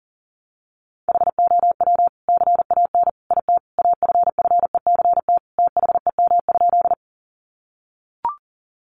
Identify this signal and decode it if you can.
Morse code, heard as a keyed tone, about 40 wpm, 700 Hz. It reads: HOW CAN IT AFFECT THEM?
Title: Sentences from Top 900 Words 40wpm